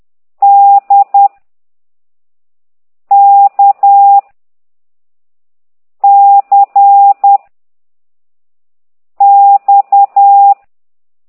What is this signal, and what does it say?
Morse code, heard as a keyed tone, about 10 wpm, 800 Hz. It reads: D K C X
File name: ct2idw / exerc4